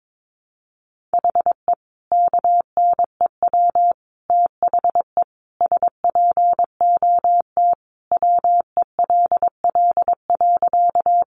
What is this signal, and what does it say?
Morse code, heard as a keyed tone, about 22 wpm, 700 Hz. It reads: HE KNEW THE SPOT WELL.
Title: Sentences from Top 1100 Words 22wpm